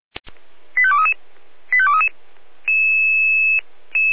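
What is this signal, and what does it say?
Five-tone selective call (ZVEI), heard as quick runs of five tones, tones of about 70 ms. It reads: 85233 85233
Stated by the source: ZVEI five-tone selective call received by SDR.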